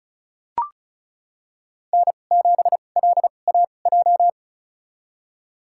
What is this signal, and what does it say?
Morse code, heard as a keyed tone, about 36 wpm, spaced as for 25 wpm, 700 Hz. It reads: N7LAJ
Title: USA Call Signs - 5 Characters - 2x Character Spacing - 35wpm